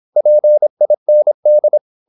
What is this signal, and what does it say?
Morse code, heard as a keyed tone, about 26 wpm, 600 Hz. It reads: PIND
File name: Korte ord DK 26wpm